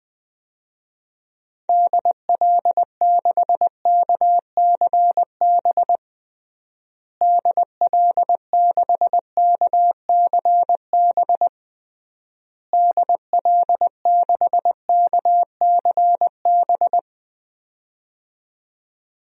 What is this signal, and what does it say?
Morse code, heard as a keyed tone, about 20 wpm, 700 Hz. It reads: DL6KCB DL6KCB DL6KCB
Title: DXCC-100 Call Signs - 6 Characters - Repeated 3x - 20wpm